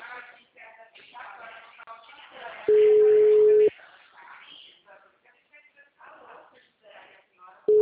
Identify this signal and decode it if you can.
Ringback tone (Europe, 425 Hz), with speech.